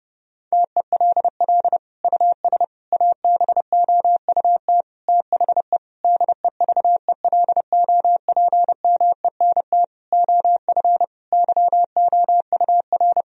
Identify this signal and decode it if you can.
Morse code, heard as a keyed tone, about 30 wpm, 700 Hz. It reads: TELL US ABOUT THE DEVELOPMENT OF YOUR
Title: Sentences from Top 1000 Words - Review Part 4 - 30wpm